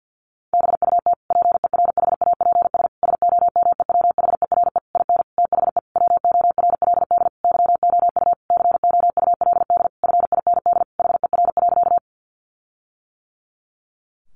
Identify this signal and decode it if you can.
Morse code, heard as a keyed tone, about 50 wpm, 700 Hz. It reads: BUT PERHAPS SOMEWHERE IN THE WORLD YOU COULD FIND HER.
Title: Sentences from Top 1400 Words 50wpm